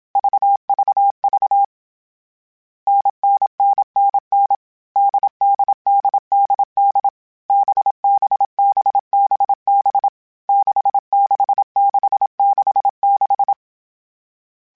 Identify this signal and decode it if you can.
Morse code, heard as a keyed tone, about 27 wpm, 800 Hz. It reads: VVV NNNNN DDDDD BBBBB 66666
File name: Short Runs / Short Runs-mp3s / NDB NDB6 @25